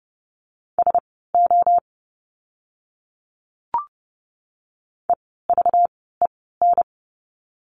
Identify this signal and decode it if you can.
Morse code, heard as a keyed tone, about 30 wpm, spaced as for 17 wpm, 700 Hz. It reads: SO EVEN
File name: course / 30:10wpm / 230 Lesson 37f - Letter x - Words 30wpm